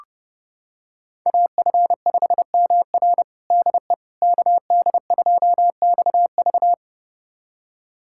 Morse code, heard as a keyed tone, about 32 wpm, 700 Hz. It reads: AF5MR DE KD2XV
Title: USA Callsign DE Callsign - 5 Characters - 30wpm